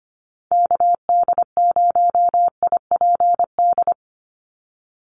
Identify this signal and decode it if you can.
Morse code, heard as a keyed tone, about 25 wpm, 700 Hz. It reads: KD0IPD
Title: USA Call Signs 25wpm